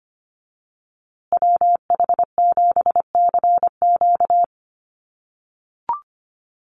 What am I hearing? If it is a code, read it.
Morse code, heard as a keyed tone, about 25 wpm, 700 Hz. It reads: WH7CQ
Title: USA Call Signs Repeated 2x 25wpm